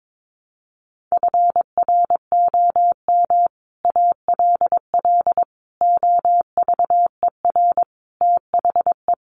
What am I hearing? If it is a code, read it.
Morse code, heard as a keyed tone, about 22 wpm, 700 Hz. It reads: FROM ALL OVER THE
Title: Sets of 5 Words Encore 22wpm